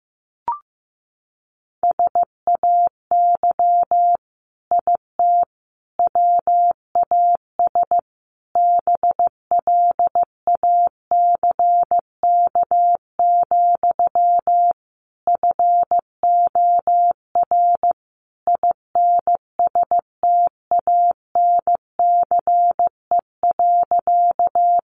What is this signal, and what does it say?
Morse code, heard as a keyed tone, about 15 wpm, 700 Hz. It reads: SAY IT WAS BLACK, FOR INSTANCE.
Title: Sentences from Top 800 Words 15wpm